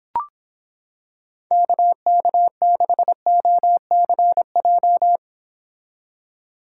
Morse code, heard as a keyed tone, about 25 wpm, 700 Hz. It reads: KK6OCJ